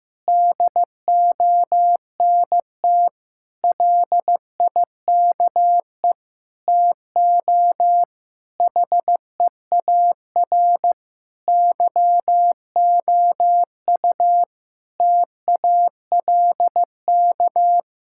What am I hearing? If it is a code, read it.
Morse code, heard as a keyed tone, about 15 wpm, 700 Hz. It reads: DONT LIKE TO HEAR YOU TALK
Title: Sentences from Top 1000 Words - Review Part 6 - 15wpm